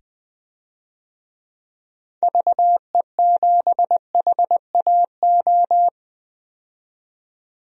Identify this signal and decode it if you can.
Morse code, heard as a keyed tone, about 20 wpm, 700 Hz. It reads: VE7HAO
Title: DXCC-100 Call Signs - 6 Characters - 20wpm